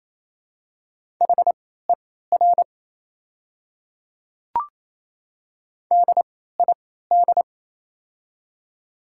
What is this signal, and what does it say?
Morse code, heard as a keyed tone, about 28 wpm, spaced as for 16 wpm, 700 Hz. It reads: HER DID
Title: 3x Character Spacing - Top 100 Words - 28wpm